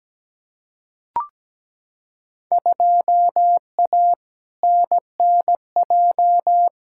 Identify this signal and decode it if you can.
Morse code, heard as a keyed tone, about 17 wpm, 700 Hz. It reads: 2A NNJ
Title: Field Day Exchanges - 17wpm